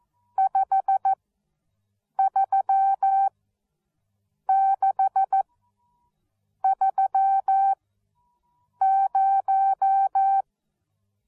Morse code, heard as a keyed tone, about 14 wpm, spaced as for 6 wpm, 780 Hz. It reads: 53630
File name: Onlinelehrgang / morsen / DJ4UF Morsen13